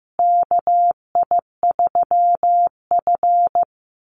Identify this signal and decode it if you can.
Morse code, heard as a keyed tone, about 15 wpm, 700 Hz. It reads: KI3F
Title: Call Signs Repeated 2x - 15wpm